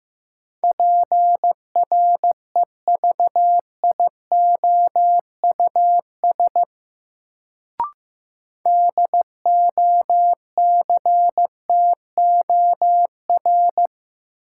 Morse code, heard as a keyed tone, about 15 wpm, 700 Hz. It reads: PREVIOUS DOCTOR
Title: Sentences from Top 1500 Words - New Word Drill - 15wpm